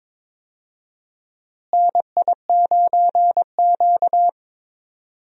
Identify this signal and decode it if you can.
Morse code, heard as a keyed tone, about 22 wpm, 700 Hz. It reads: NI9Q
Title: USA Call Signs - 4 Characters - Repeated 2x - 22wpm